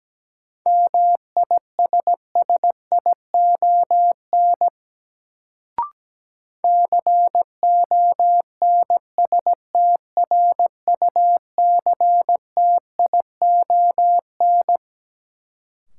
Morse code, heard as a keyed tone, about 17 wpm, 700 Hz. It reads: MISSION CONSTRUCTION
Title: Sentences from Top 1600 Words - New Word Drill Alt - 17wpm